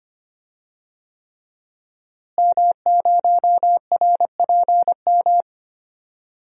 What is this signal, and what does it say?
Morse code, heard as a keyed tone, about 25 wpm, 700 Hz. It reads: M0RPM
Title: DXCC-100 Call Signs - 5 Characters - Repeated 3x - 25wpm